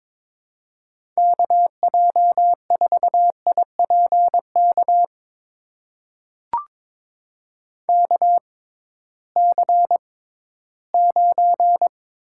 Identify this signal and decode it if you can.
Morse code, heard as a keyed tone, about 22 wpm, spaced as for 14 wpm, 700 Hz. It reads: KJ4IPK KC9
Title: USA Call Signs - 6 Characters - 6x Character Spacing - 22wpm